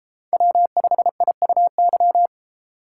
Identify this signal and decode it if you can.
Morse code, heard as a keyed tone, about 30 wpm, 700 Hz. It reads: W5IUY